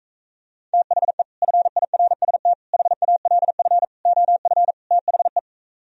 Morse code, heard as a keyed tone, about 45 wpm, 700 Hz. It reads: THE FIRST HALF OF THE